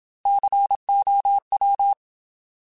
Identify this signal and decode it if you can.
Morse code, heard as a keyed tone, about 27 wpm, 800 Hz. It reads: COW
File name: Alphabetical / Alphabetical mp3s / Cs @25